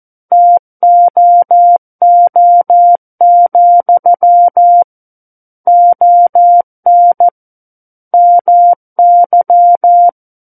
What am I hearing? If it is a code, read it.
Morse code, heard as a keyed tone, about 14 wpm, 700 Hz. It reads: TOO, ON MY
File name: Book One / 05-Chapter 5 - 14 WPM